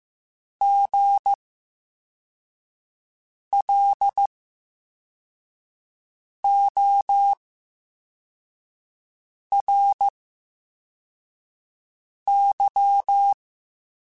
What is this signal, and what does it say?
Morse code, heard as a keyed tone, about 15 wpm, spaced as for 4 wpm, 800 Hz. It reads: GLORY
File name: Special INGs-mp3s / Special INGs-mp3s / VerySlowToFaster Ing @13